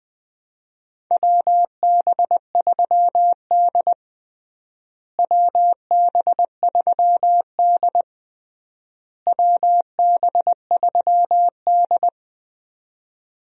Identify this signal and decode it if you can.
Morse code, heard as a keyed tone, about 20 wpm, 700 Hz. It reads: WB3D WB3D WB3D
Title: USA Call Signs - 4 Characters - Repeated 3x - 20wpm